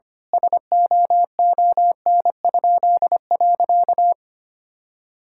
Morse code, heard as a keyed tone, about 25 wpm, 700 Hz. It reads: SOON?.
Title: Pangram Challenge v2 - 25wpm